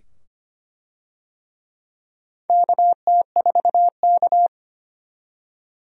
Morse code, heard as a keyed tone, about 25 wpm, 700 Hz. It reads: KT4K
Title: USA Call Signs - 4 Characters - 25wpm